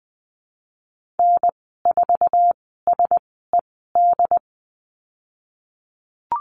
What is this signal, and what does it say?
Morse code, heard as a keyed tone, about 20 wpm, spaced as for 15 wpm, 700 Hz. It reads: N4SED
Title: 076 Lesson 13g - Letter L - Call Signs 20wpm